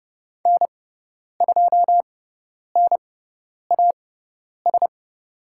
Morse code, heard as a keyed tone, about 30 wpm, spaced as for 10 wpm, 700 Hz. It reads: N2NAS
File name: Lesson 16g - Letter c - Call Signs 30wpm